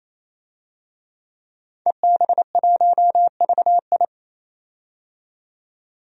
Morse code, heard as a keyed tone, about 29 wpm, 690 Hz. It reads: EB1VI
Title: DXCC-100 Call Signs - 5 Characters - 28wpm